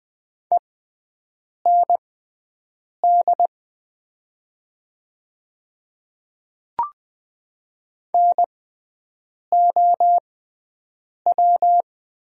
Morse code, heard as a keyed tone, about 20 wpm, spaced as for 7 wpm, 700 Hz. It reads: END NOW
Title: 6x Character Spacing - Top 300 Words - 20wpm